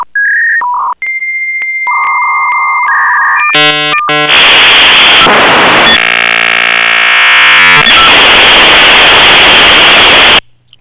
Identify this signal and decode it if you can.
Dial-up modem handshake.